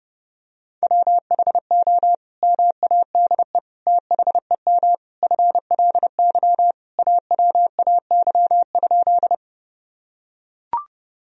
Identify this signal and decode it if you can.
Morse code, heard as a keyed tone, about 32 wpm, 700 Hz. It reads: WHO MADE THEM FLY AWAY?
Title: Sentences from Top 1000 Words - Review Part 2 - 30wpm